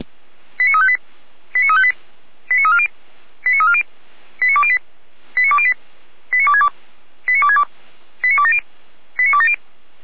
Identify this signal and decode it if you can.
Five-tone selective call (ZVEI), heard as quick runs of five tones, tones of about 70 ms. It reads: 89358 89358 89340 89340 89208 89208 89262 89262 89270 89270